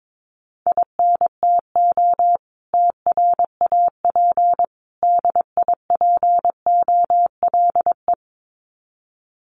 Morse code, heard as a keyed tone, about 22 wpm, 700 Hz. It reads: INTO TRAP DIPOLE
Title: Common QSO Phrases 22wpm